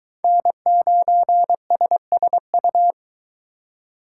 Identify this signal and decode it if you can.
Morse code, heard as a keyed tone, about 22 wpm, 700 Hz. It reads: N9SSU